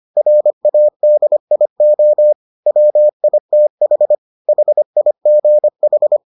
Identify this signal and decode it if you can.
Morse code, heard as a keyed tone, about 25 wpm, 600 Hz. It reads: RADIO WITH HIGH